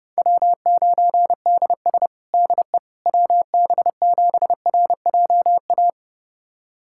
Morse code, heard as a keyed tone, about 32 wpm, 700 Hz. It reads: W9DS DE WB7RJA
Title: USA Callsign DE Callsign 30wpm